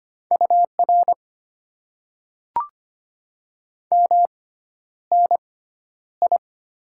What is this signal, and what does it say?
Morse code, heard as a keyed tone, about 25 wpm, spaced as for 11 wpm, 700 Hz. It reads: UR MNI
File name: Top 90 QSO Elements - 6x Character Spacing - 25wpm